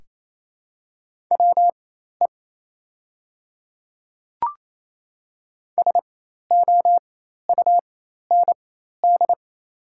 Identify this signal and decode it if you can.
Morse code, heard as a keyed tone, about 28 wpm, spaced as for 13 wpm, 690 Hz. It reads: WE SOUND
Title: 4x Character Spacing - Top 100 Words - 28wpm